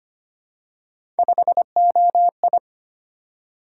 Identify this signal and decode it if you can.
Morse code, heard as a keyed tone, about 25 wpm, 700 Hz. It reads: 5OI
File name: Three Letter-Number Combinations - 25wpm